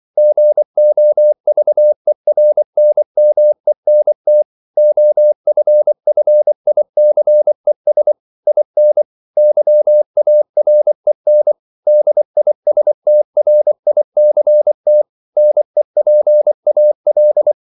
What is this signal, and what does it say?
Morse code, heard as a keyed tone, about 24 wpm, 600 Hz. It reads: GOVERNMENT OFFICES IN YAREN DISTRICT NEPAL
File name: CW / Practice / NationalCapitals24